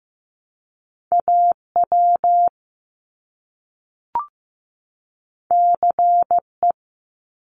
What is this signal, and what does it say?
Morse code, heard as a keyed tone, about 15 wpm, 700 Hz. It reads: AW CE
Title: Top Two Letter Combinations in Words 15wpm